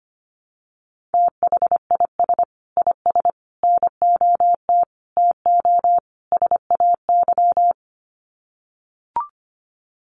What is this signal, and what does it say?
Morse code, heard as a keyed tone, about 25 wpm, 700 Hz. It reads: THIS IS NOT TO SAY
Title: Sets of 5 Words Encore 25wpm